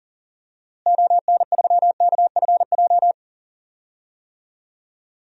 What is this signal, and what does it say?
Morse code, heard as a keyed tone, about 40 wpm, 700 Hz. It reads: ON3KFJ
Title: DXCC-100 Call Signs - 6 Characters - Repeated 3x - 40wpm